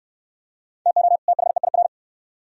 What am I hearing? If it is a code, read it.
Morse code, heard as a keyed tone, about 65 wpm, 690 Hz. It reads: TO THEIR